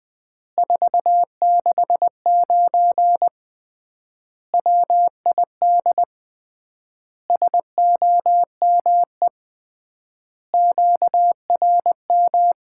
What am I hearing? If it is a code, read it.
Morse code, heard as a keyed tone, about 21 wpm, 700 Hz. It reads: 469 WID SOME QRM
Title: Common QSO Phrases - 3x Word-Spacing - 20wpm